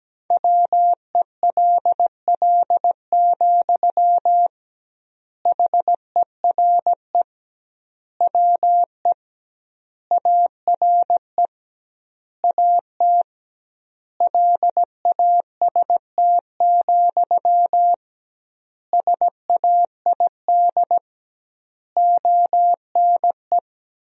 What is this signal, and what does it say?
Morse code, heard as a keyed tone, about 17 wpm, 700 Hz. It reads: WELL, HERE WE ARE AT LAST, SAID ONE
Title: 2x Word-Spacing - Sentences from Top 300 Words - 17wpm